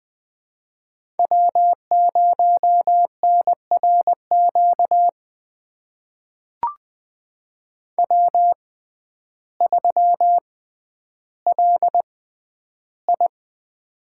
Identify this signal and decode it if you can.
Morse code, heard as a keyed tone, about 20 wpm, spaced as for 11 wpm, 700 Hz. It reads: W0NRQ W3LI
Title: USA Call Signs - 6x Character Spacing - 20wpm